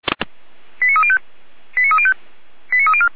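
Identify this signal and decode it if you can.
Five-tone selective call (ZVEI), heard as quick runs of five tones, tones of about 70 ms. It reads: 89305 89305 89304
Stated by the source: ZVEI five-tone call 89208 received by SDR.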